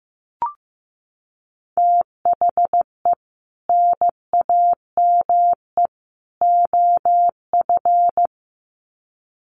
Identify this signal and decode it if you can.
Morse code, heard as a keyed tone, about 15 wpm, 700 Hz. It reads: THE NAME OF
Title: Sets of 3 Words 15wpm